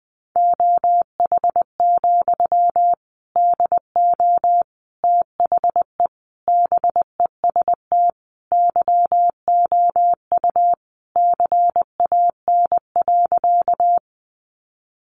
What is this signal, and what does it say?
Morse code, heard as a keyed tone, about 20 wpm, 700 Hz. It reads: OH, DO THE BEST YOU CAN.
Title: Sentences from Top 500 Words 20wpm